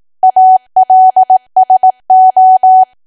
Morse code, heard as a keyed tone, about 18 wpm, 750 Hz. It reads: ALSO
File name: Archive / 18 WPM / 18WPM